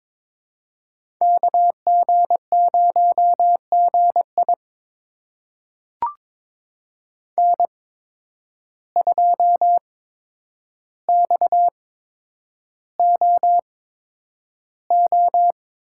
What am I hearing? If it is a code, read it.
Morse code, heard as a keyed tone, about 21 wpm, spaced as for 9 wpm, 700 Hz. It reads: KG0GI N2XOO
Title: USA Call Signs - 5 Characters - 8x Character Spacing - 22wpm